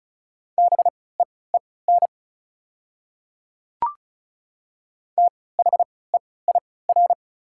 Morse code, heard as a keyed tone, about 34 wpm, spaced as for 20 wpm, 700 Hz. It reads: BEEN THEIR
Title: 3x Character Spacing - Top 200 Words - 35wpm